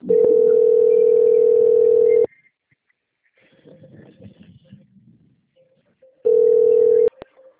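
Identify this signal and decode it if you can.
Ringback tone (North America), 440 + 480 Hz.
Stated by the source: Telephone call recording from the Rio Claro 3 system.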